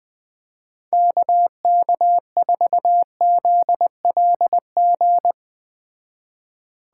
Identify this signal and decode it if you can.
Morse code, heard as a keyed tone, about 21 wpm, 700 Hz. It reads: KK4ZLG